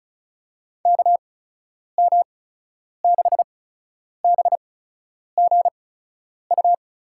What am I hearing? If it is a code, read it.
Morse code, heard as a keyed tone, about 36 wpm, spaced as for 10 wpm, 700 Hz. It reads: KM6BGU
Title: USA Call Signs - 6 Characters - 8x Character Spacing - 35wpm